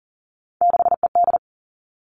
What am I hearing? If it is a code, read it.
Morse code, heard as a keyed tone, about 40 wpm, 700 Hz. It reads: 6ED
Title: All Cut Numbers - 3-Digit 40wpm